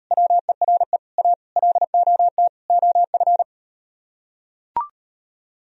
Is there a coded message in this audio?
Yes: Morse code from 0.1 to 3.4 s, heard as a keyed tone, about 38 wpm, 700 Hz. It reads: WERE A LOT OF